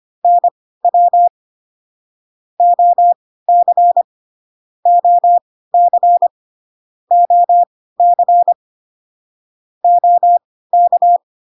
Morse code, heard as a keyed tone, about 25 wpm, spaced as for 16 wpm, 700 Hz. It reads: NW OC OC OC OK